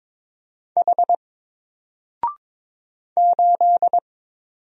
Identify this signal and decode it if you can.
Morse code, heard as a keyed tone, about 22 wpm, 700 Hz. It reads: H 8
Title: Single Letter-Number Alt 22wpm